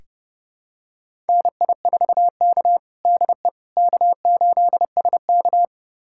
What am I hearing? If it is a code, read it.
Morse code, heard as a keyed tone, about 29 wpm, 700 Hz. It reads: NI4K DE K8SK
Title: USA Callsign DE Callsign - 4 Characters - 30wpm